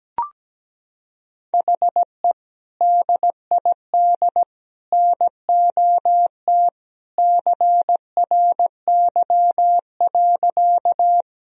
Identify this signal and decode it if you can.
Morse code, heard as a keyed tone, about 17 wpm, 700 Hz. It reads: HE DID NOT CRY.
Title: Sentences from Top 1000 Words - Review Part 2 - 17wpm